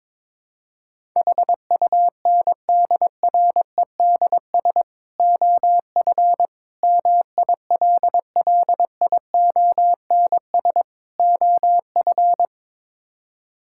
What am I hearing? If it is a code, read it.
Morse code, heard as a keyed tone, about 22 wpm, 700 Hz. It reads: HUNDREDS OF MILLIONS OF
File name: practice / 22wpm / Build A Phrase - 5-Word Phrases - 22wpm